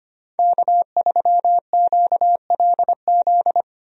Morse code, heard as a keyed tone, about 25 wpm, 700 Hz. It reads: K3QLZ
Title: USA Call Signs - 5 Characters - Repeated 2x - 25wpm